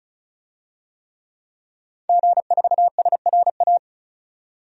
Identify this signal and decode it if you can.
Morse code, heard as a keyed tone, about 34 wpm, 700 Hz. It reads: G4SRA
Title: DXCC-100 Call Signs - 5 Characters - 35wpm